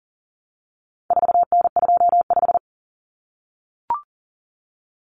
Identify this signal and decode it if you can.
Morse code, heard as a keyed tone, about 40 wpm, 700 Hz. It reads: 4N25
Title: Common Cut Numbers - 4-Digit 40wpm